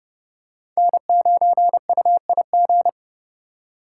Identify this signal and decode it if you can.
Morse code, heard as a keyed tone, about 29 wpm, 700 Hz. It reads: N9UIG